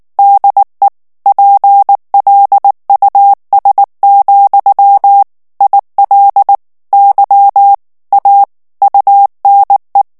Morse code, heard as a keyed tone, about 19 wpm, 800 Hz. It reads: DE PLUS, IL Y A UNE
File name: morse mp3 / FRANCAIS TEXTE CLAIR 1 VITESSE 18 MOTS MINUTES